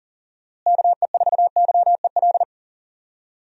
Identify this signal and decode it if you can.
Morse code, heard as a keyed tone, about 40 wpm, 700 Hz. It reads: KE4YEL
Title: USA Call Signs - 6 Characters - Repeated 3x - 40wpm